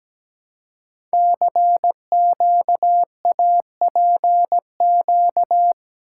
Morse code, heard as a keyed tone, about 17 wpm, 700 Hz. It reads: CQAPQ
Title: Code Groups - 5 Letters - 17wpm